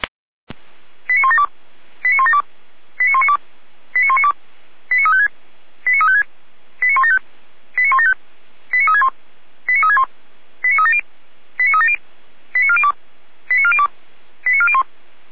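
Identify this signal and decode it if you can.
Five-tone selective call (ZVEI), heard as quick runs of five tones, tones of about 80 ms. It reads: 89172 89172 89192 89192 89356 89356 89175 89175 89361 89361 89370 89370 89402 89402 89401
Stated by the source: A ZVEI five-tone selective call, code 89409, received with an SDR.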